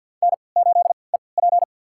Morse code, heard as a keyed tone, about 53 wpm, spaced as for 29 wpm, 700 Hz. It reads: N8EP
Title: USA Call Signs - 4 Characters - 3x Character Spacing - 50wpm